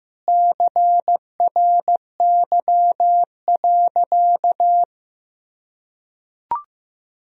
Morse code, heard as a keyed tone, about 15 wpm, 700 Hz. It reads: CRY.
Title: Sentences from Top 1000 Words - Review Part 2 - 15wpm